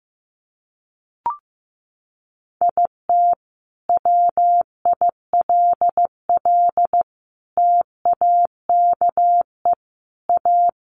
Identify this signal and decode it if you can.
Morse code, heard as a keyed tone, about 15 wpm, 700 Hz. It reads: IT WILL TAKE A
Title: Sentences from Top 200 Words 15wpm